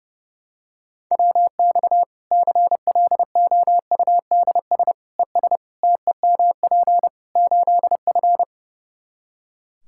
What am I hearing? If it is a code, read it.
Morse code, heard as a keyed tone, about 30 wpm, 700 Hz. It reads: WX CLOUDS ES TEMP 8F
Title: Practice QSO - Exchange 2 - Info - 30wpm